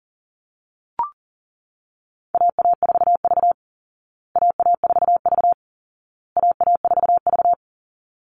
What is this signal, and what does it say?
Morse code, heard as a keyed tone, about 40 wpm, 700 Hz. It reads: AA4V AA4V AA4V
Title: Call Signs Repeated 2x - 40wpm